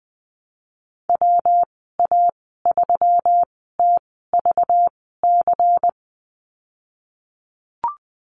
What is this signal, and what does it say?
Morse code, heard as a keyed tone, about 19 wpm, spaced as for 14 wpm, 700 Hz. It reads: WA3TVC
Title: 213 Lesson 34g - Number 8 - Call Signs 20wpm